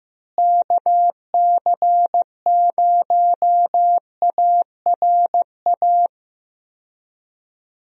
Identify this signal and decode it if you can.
Morse code, heard as a keyed tone, about 15 wpm, 700 Hz. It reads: KC0ARA
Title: USA Call Signs - 6 Characters - Repeated 2x - 15wpm